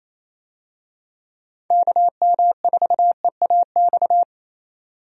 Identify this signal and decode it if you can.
Morse code, heard as a keyed tone, about 27 wpm, 700 Hz. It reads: KM4EAX
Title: USA Call Signs - 6 Characters - 28wpm